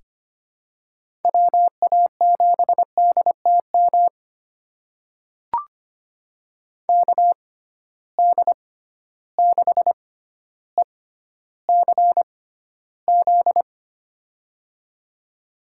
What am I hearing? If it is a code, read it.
Morse code, heard as a keyed tone, about 25 wpm, spaced as for 13 wpm, 700 Hz. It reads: WA7DTM KD6ECZ
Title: USA Call Signs - 6 Characters - 6x Character Spacing - 25wpm